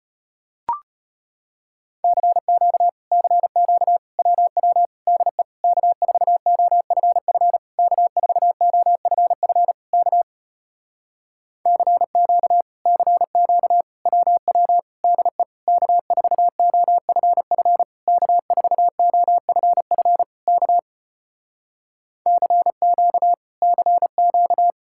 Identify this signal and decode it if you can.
Morse code, heard as a keyed tone, about 33 wpm, 700 Hz. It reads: CQ CQ WW DE K4OFF K4OFF K CQ CQ WW DE K4OFF K4OFF K CQ CQ